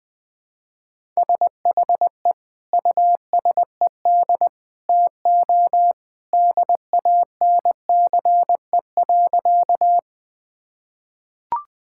Morse code, heard as a keyed tone, about 21 wpm, 700 Hz. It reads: SHE USED TO DANCE.
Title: Sentences with 2-Word Phrases - 20wpm